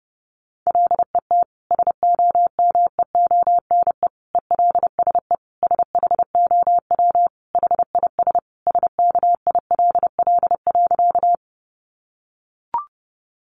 Morse code, heard as a keyed tone, about 28 wpm, 700 Hz. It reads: LET SOMEONE ELSE SHOW HIS SKILL.
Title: Sentences from Top 1200 Words 30wpm